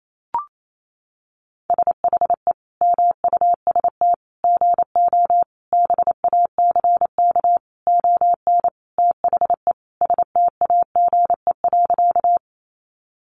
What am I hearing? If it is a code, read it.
Morse code, heard as a keyed tone, about 28 wpm, 700 Hz. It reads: SHE MUST GO BACK ON THE STAGE.
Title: Sentences from Top 1200 Words 28wpm